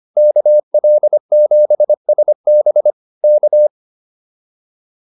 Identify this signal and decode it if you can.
Morse code, heard as a keyed tone, about 25 wpm, 600 Hz. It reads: KL7SB K